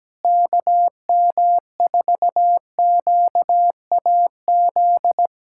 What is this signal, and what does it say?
Morse code, heard as a keyed tone, about 17 wpm, 700 Hz. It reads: KM4QAZ
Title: USA Callsign DE Callsign - 6 Characters - 17wpm